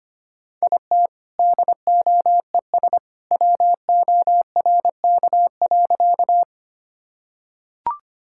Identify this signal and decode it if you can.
Morse code, heard as a keyed tone, about 25 wpm, 690 Hz. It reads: IT DOES WORK.
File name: practice / 25wpm / Sentences with 2-Word Phrases - 25wpm